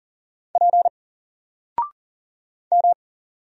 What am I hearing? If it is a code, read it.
Morse code, heard as a keyed tone, about 40 wpm, 700 Hz. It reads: P M
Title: Single Letter-Number Alt 40wpm